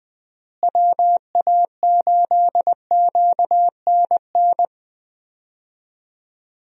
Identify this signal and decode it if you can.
Morse code, heard as a keyed tone, about 20 wpm, 700 Hz. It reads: WA8QNN
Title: USA Call Signs - 6 Characters - Repeated 3x - 20wpm